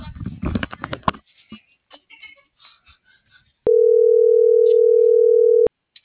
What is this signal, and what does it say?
Ringback tone (North America), 440 + 480 Hz.